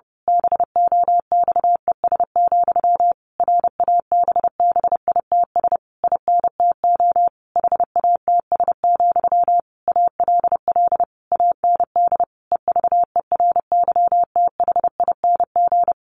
Morse code, heard as a keyed tone, about 30 wpm, 700 Hz. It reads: BOXES, RABBITS INTO HATS, ALL AND EVERYTHING
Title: A Sound Of Thunder 30wpm